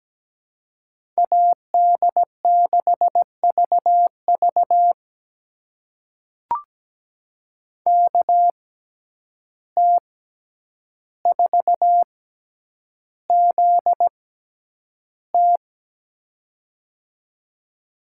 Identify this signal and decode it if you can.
Morse code, heard as a keyed tone, about 17 wpm, spaced as for 9 wpm, 700 Hz. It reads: AD6VV KT4ZT
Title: USA Call Signs - 5 Characters - 6x Character Spacing - 17wpm